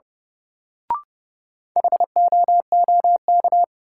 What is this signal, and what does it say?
Morse code, heard as a keyed tone, about 32 wpm, 700 Hz. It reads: HOOK
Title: Sentences from Top 1600 Words - New Word Drill - 30wpm